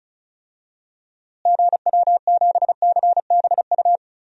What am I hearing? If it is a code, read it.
Morse code, heard as a keyed tone, about 34 wpm, 700 Hz. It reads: GW7CBU
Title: DXCC-100 Call Signs - 6 Characters - Repeated 2x - 35wpm